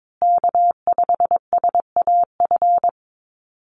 Morse code, heard as a keyed tone, about 22 wpm, 700 Hz. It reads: K5SAF
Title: USA Call Signs Repeated 2x 22wpm